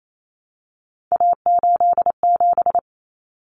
Morse code, heard as a keyed tone, about 28 wpm, 700 Hz. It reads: A87
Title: Common Cut Numbers - 3-Digit 28wpm